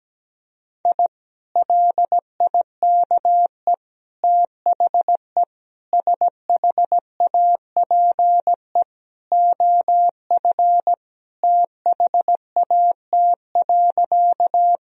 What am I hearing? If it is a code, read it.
Morse code, heard as a keyed tone, about 17 wpm, 700 Hz. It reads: I LIKE THE SHAPE OF THAT.
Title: Sentences from Top 1000 Words - Review Part 3 - 17wpm